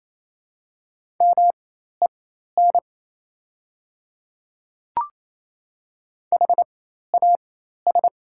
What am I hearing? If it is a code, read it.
Morse code, heard as a keyed tone, about 28 wpm, spaced as for 13 wpm, 700 Hz. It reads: MEN HAS
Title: 4x Character Spacing - Top 500 Words - 28wpm